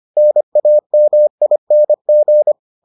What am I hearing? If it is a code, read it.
Morse code, heard as a keyed tone, about 25 wpm, 600 Hz. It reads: NAMING